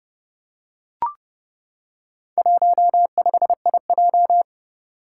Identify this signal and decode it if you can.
Morse code, heard as a keyed tone, about 29 wpm, 700 Hz. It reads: 15IJ
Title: Four Challenging Letter-Number Combinations 30wpm